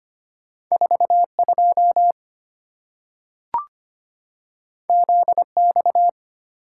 Morse code, heard as a keyed tone, about 25 wpm, 690 Hz. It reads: 42 ZX
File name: Two Letter-Number Combinations - 25wpm